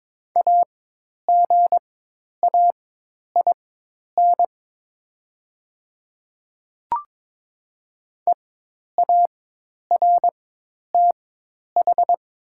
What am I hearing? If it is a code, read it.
Morse code, heard as a keyed tone, about 22 wpm, spaced as for 10 wpm, 700 Hz. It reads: AGAIN EARTH
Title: 4x Character Spacing - Top 300 Words - 22wpm